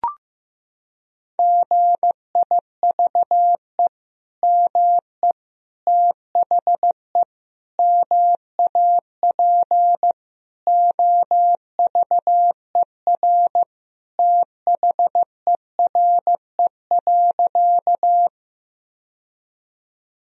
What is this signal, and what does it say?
Morse code, heard as a keyed tone, about 15 wpm, 700 Hz. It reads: GIVE ME THE MAP OVER THERE.
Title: Sentences from Top 1000 Words - Review Part 2 - 15wpm